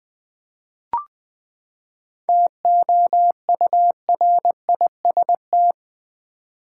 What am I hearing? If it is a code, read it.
Morse code, heard as a keyed tone, about 20 wpm, 700 Hz. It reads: TOURIST
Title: Seven Letter Words - 20wpm